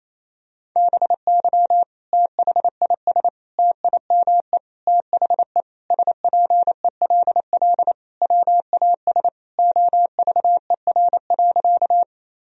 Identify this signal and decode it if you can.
Morse code, heard as a keyed tone, about 28 wpm, 700 Hz. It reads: BY THIS TIME THE SPELL WAS OVER.
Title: Sentences from Top 1000 Words - Review Part 1 - 28wpm